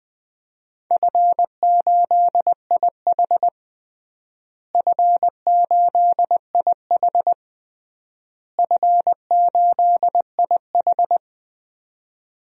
Morse code, heard as a keyed tone, about 21 wpm, 700 Hz. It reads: F8IH F8IH F8IH
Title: DXCC-100 Call Signs - 4 Characters - Repeated 3x - 20wpm